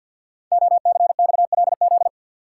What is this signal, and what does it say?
Morse code, heard as a keyed tone, about 53 wpm, 700 Hz. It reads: OCXLZ
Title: Code Groups - 5 Letters - 50wpm